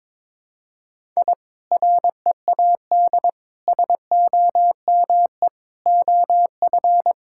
Morse code, heard as a keyed tone, about 22 wpm, 700 Hz. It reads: I READ SOME OF
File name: practice / 22wpm / Sentences with 2-Word Phrases - 22wpm